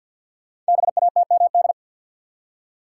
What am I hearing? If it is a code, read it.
Morse code, heard as a keyed tone, about 51 wpm, 700 Hz. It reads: BATMD
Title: Code Groups - 5 Letters - 50wpm